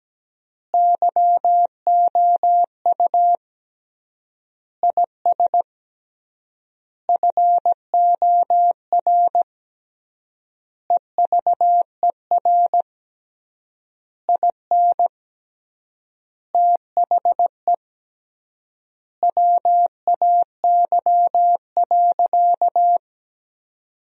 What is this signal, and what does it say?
Morse code, heard as a keyed tone, about 17 wpm, 700 Hz. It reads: YOU IS FOR EVER IN THE WAY.
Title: 3x Word-Spacing - Sentences from Top 400 Words - 17wpm